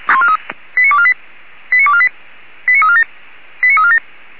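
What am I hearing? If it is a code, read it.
Five-tone selective call (ZVEI), heard as quick runs of five tones, tones of about 70 ms. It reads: 89248 89248 89347 89347
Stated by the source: ZVEI five-tone selective call 89360, received by SDR.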